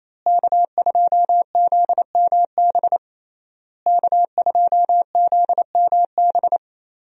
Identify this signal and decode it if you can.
Morse code, heard as a keyed tone, about 28 wpm, 700 Hz. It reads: K2ZMB K2ZMB
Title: USA Call Signs - 5 Characters - Repeated 3x - 28wpm